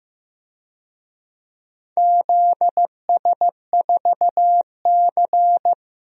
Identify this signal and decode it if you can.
Morse code, heard as a keyed tone, about 16 wpm, 700 Hz. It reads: ZS4C
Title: DXCC-100 Call Signs - 4 Characters - Repeated 3x - 15wpm